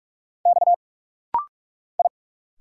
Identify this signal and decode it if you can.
Morse code, heard as a keyed tone, about 46 wpm, 700 Hz. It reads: X I
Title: Single Letter-Number Alt - Rapid-Fire 45wpm